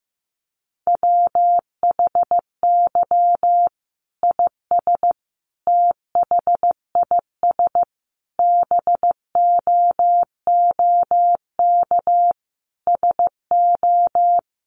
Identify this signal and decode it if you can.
Morse code, heard as a keyed tone, about 15 wpm, 700 Hz. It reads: WHY IS THIS BOOK SO
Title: Sentences from Top 1200 Words 15wpm